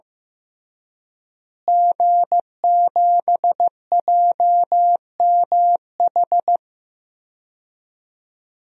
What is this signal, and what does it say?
Morse code, heard as a keyed tone, about 15 wpm, 700 Hz. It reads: G7JMH
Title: DXCC-100 Call Signs - 5 Characters - Repeated 3x - 15wpm